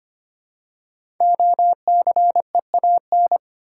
Morse code, heard as a keyed tone, about 25 wpm, 700 Hz. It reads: OCEAN
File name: 5 Letter Words - First 3 Letters Given - 25wpm